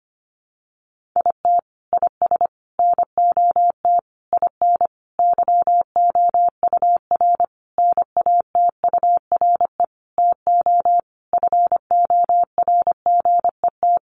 Morse code, heard as a keyed tone, about 25 wpm, 700 Hz. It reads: IT IS NOT IN YOUR NATURE TO FORGET
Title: Sentences from Top 1200 Words 25wpm